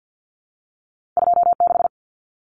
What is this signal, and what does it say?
Morse code, heard as a keyed tone, about 50 wpm, 700 Hz. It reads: 26
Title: Common Cut Numbers - 2-Digit 50wpm